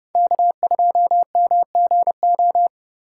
Morse code, heard as a keyed tone, about 30 wpm, 700 Hz. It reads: K2MGO
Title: USA Call Signs - 3x Character Spacing - 30wpm